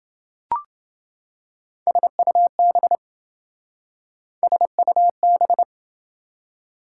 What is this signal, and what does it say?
Morse code, heard as a keyed tone, about 27 wpm, 700 Hz. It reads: SUB SUB